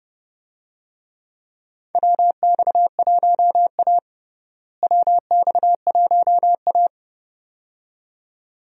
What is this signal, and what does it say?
Morse code, heard as a keyed tone, about 30 wpm, 700 Hz. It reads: WX1A WX1A
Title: USA Call Signs - 4 Characters - Repeated 2x - 30wpm